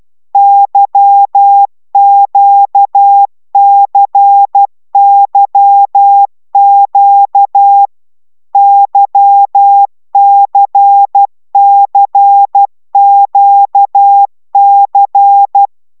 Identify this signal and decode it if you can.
Morse code, heard as a keyed tone, about 12 wpm, 800 Hz. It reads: YQCYQ YCCQC
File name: taviro tanulas / cw 75 15 10